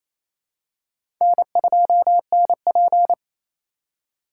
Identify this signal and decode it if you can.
Morse code, heard as a keyed tone, about 27 wpm, 700 Hz. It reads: N2NP
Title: USA Call Signs - 4 Characters - 4x Character Spacing - 28wpm